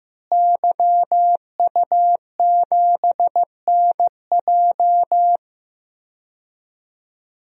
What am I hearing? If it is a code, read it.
Morse code, heard as a keyed tone, about 15 wpm, 700 Hz. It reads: YU7NJ
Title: DXCC-100 Call Signs - 5 Characters - Repeated 2x - 15wpm